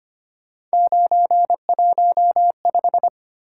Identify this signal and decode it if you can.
Morse code, heard as a keyed tone, about 25 wpm, 700 Hz. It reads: 915
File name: Three Letter-Number Combinations - 25wpm